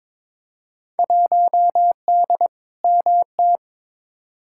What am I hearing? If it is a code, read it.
Morse code, heard as a keyed tone, about 22 wpm, 700 Hz. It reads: 1D MT
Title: Field Day Exchanges - 22wpm